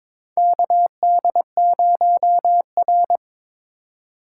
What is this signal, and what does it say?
Morse code, heard as a keyed tone, about 22 wpm, 700 Hz. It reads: KD0R
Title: USA Call Signs - 4 Characters - 4x Character Spacing - 22wpm